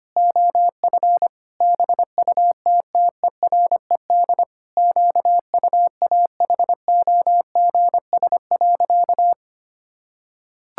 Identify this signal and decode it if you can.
Morse code, heard as a keyed tone, about 25 wpm, 700 Hz. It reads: OF BUTTERED QUAHOGS.
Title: Pangram Challenge v2 - 25wpm